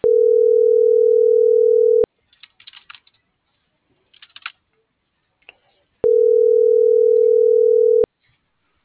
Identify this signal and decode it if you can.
Ringback tone (North America), 440 + 480 Hz.